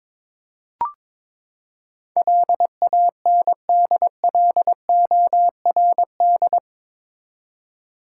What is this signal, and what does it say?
Morse code, heard as a keyed tone, about 22 wpm, 700 Hz. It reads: LANDLORD
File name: practice / 22wpm / Eight Letter Words - 22wpm